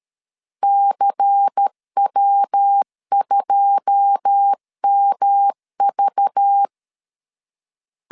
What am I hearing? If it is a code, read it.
Morse code, heard as a keyed tone, about 13 wpm, 800 Hz. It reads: CW2MV
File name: CW / DET / 1 9 1500C S A 12wpm 3800